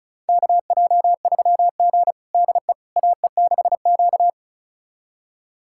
Morse code, heard as a keyed tone, about 34 wpm, 700 Hz. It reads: KJ3G DE AE6Q
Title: USA Callsign DE Callsign - 4 Characters - 35wpm